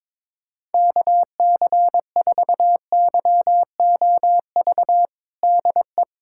Morse code, heard as a keyed tone, about 22 wpm, 700 Hz. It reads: KC4YOV DE
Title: USA Callsign DE Callsign - 6 Characters - 22wpm